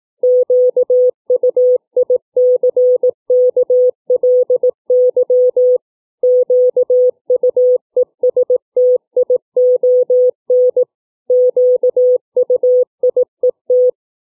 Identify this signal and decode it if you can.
Morse code, heard as a keyed tone, about 18 wpm, 500 Hz. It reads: QUICKLY QUESTION QUIET
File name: CW / letter filtering 2